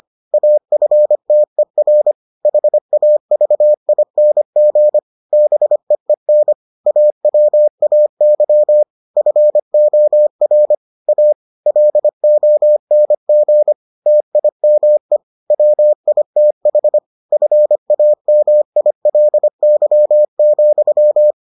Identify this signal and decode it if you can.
Morse code, heard as a keyed tone, about 25 wpm, 600 Hz. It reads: AFTER HAVING BEEN AWAY FOR A LONG TIME WITH FAMILY,